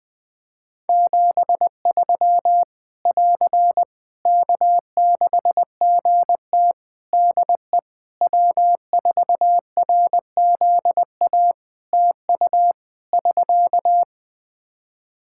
Morse code, heard as a keyed tone, about 20 wpm, 700 Hz. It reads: 73 <AR> K6GT DE W4RZA TU <SK>
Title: Practice QSO - Exchange 4 - Ending - 20wpm